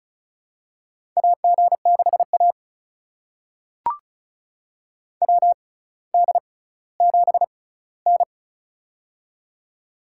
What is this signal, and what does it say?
Morse code, heard as a keyed tone, about 36 wpm, spaced as for 18 wpm, 700 Hz. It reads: AG6A WD7N
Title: USA Call Signs - 6x Character Spacing - 35wpm